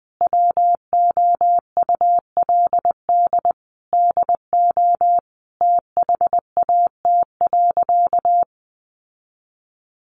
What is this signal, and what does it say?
Morse code, heard as a keyed tone, about 20 wpm, 700 Hz. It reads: WOULD DO THAT.
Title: Sentences from Top 800 Words 20wpm